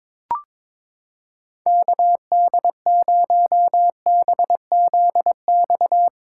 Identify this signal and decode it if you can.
Morse code, heard as a keyed tone, about 21 wpm, 700 Hz. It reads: KD0BZX
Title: USA Callsign DE Callsign - 6 Characters - 22wpm